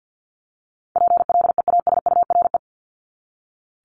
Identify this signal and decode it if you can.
Morse code, heard as a keyed tone, about 48 wpm, 700 Hz. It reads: PLEASURE
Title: Top 1501-1750 Words 50wpm